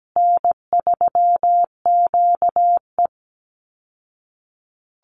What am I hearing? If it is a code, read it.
Morse code, heard as a keyed tone, about 17 wpm, 700 Hz. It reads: N3QE
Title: Easy Call Signs Repeated 3x - 17wpm